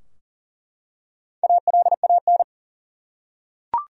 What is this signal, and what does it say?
Morse code, heard as a keyed tone, about 40 wpm, 700 Hz. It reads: ALAN
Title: USA Common Names - Max 4 Letters - 40wpm